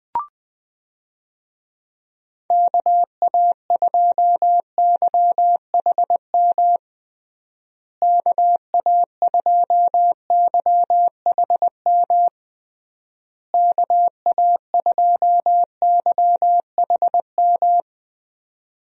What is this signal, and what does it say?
Morse code, heard as a keyed tone, about 21 wpm, 700 Hz. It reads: KA2YHM KA2YHM KA2YHM
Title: USA Call Signs - 6 Characters - Repeated 3x - 20wpm